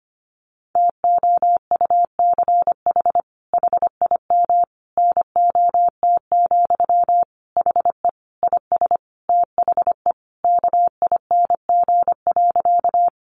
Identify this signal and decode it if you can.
Morse code, heard as a keyed tone, about 25 wpm, 700 Hz. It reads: TOUCH HIM NOT, HE IS THE KING.
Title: Sentences from Top 1200 Words 25wpm